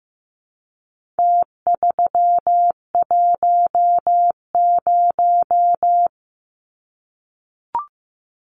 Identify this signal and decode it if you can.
Morse code, heard as a keyed tone, about 15 wpm, 700 Hz. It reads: T310
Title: Numbers - Contest Style - 4-Digit 15wpm